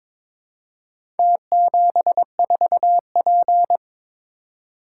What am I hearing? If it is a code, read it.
Morse code, heard as a keyed tone, about 22 wpm, 700 Hz. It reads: T74P
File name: practice / 22wpm / Four Letter-Number Combinations - 22wpm